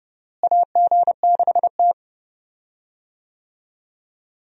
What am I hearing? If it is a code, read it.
Morse code, heard as a keyed tone, about 32 wpm, 700 Hz. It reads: AG6T
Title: USA Call Signs - 4 Characters - Repeated 2x - 30wpm